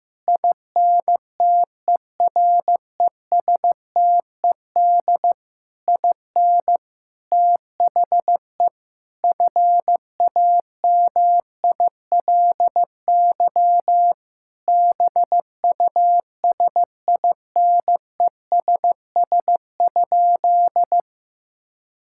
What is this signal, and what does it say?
Morse code, heard as a keyed tone, about 15 wpm, 700 Hz. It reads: INTERESTED IN THE FAMILY BUSINESS?
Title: Sentences from Top 1000 Words - Review Part 6 - 15wpm